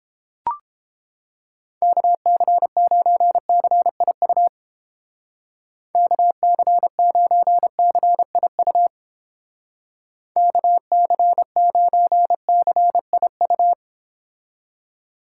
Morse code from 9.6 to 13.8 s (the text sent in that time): KC9CIU